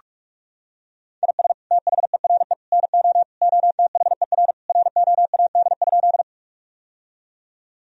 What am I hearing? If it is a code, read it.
Morse code, heard as a keyed tone, about 46 wpm, 700 Hz. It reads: IS THERE NO OTHER ROAD?
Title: Sentences from Top 1000 Words - Review Part 2 - 45wpm